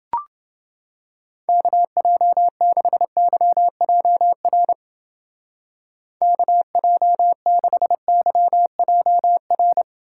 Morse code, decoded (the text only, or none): KJ6YJR KJ6YJR